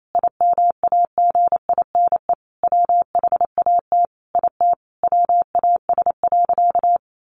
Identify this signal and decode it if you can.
Morse code, heard as a keyed tone, about 28 wpm, 700 Hz. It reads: IMAGINE WHAT IT WAS.
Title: Sentences from Top 1400 Words 28wpm